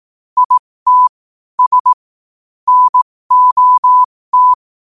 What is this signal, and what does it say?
Morse code, heard as a keyed tone, about 18 wpm, spaced as for 15 wpm, 1000 Hz. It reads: IT S NOT
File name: qotdfiles / 15wpm